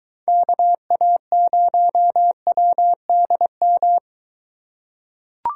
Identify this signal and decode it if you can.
Morse code, heard as a keyed tone, about 23 wpm, 700 Hz. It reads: KA0WDM